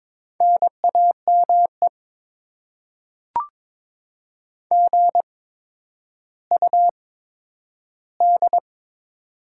Morse code, heard as a keyed tone, about 22 wpm, spaced as for 11 wpm, 700 Hz. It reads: NAME GUD
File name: Top 20 QSO Elements - 8x Character Spacing - 22wpm